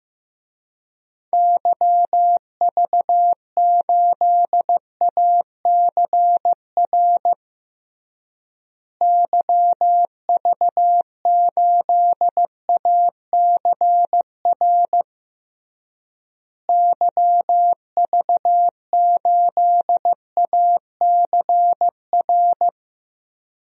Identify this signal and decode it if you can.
Morse code, heard as a keyed tone, about 15 wpm, 700 Hz. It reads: YV8ACR YV8ACR YV8ACR
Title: DXCC-100 Call Signs - 6 Characters - Repeated 3x - 15wpm